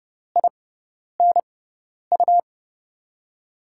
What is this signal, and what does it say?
Morse code, heard as a keyed tone, about 30 wpm, spaced as for 10 wpm, 700 Hz. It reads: INU
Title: USA Call Signs - 6 Characters - 6x Character Spacing - 30wpm